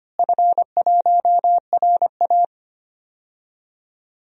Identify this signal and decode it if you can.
Morse code, heard as a keyed tone, about 25 wpm, 700 Hz. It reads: F1RA
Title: DXCC-100 Call Signs - 4 Characters - Repeated 2x - 25wpm